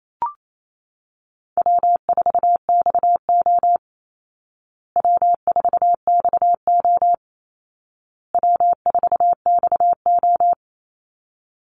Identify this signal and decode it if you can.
Morse code, heard as a keyed tone, about 28 wpm, 700 Hz. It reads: W4XO W4XO W4XO
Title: Easy Call Signs Repeated 3x - 28wpm